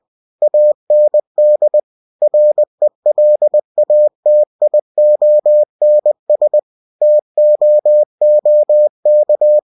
Morse code, heard as a keyed tone, about 20 wpm, 600 Hz. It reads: AND RELATIONS TOOK
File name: cw-books / winnie-the-pooh / winnie-20wpm-page-0019